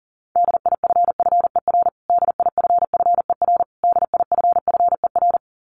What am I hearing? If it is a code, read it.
Morse code, heard as a keyed tone, about 40 wpm, 700 Hz. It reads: DIFFER DIFFER DIFFER
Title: Top 500 Words Repeated 40wpm